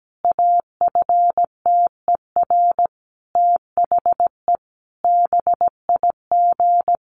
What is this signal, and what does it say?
Morse code, heard as a keyed tone, about 17 wpm, 700 Hz. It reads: AFTER THE BIG
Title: Sentences from Top 1500 Words 17wpm